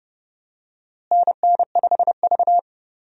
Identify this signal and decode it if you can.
Morse code, heard as a keyed tone, about 32 wpm, 700 Hz. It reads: NN5V
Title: USA Call Signs - 4 Characters - Repeated 3x - 30wpm